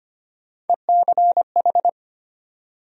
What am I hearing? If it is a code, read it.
Morse code, heard as a keyed tone, about 25 wpm, 700 Hz. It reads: ECH
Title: USA Call Sign Postfixes - 3 Letters 25wpm